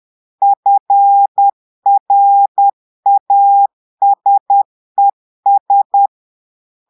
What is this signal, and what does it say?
Morse code, heard as a keyed tone, about 10 wpm, 800 Hz. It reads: FRASES
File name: portugais texte en clair 2 vitesse 10 mots minutes